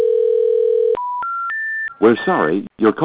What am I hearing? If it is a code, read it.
Special information tone, with speech.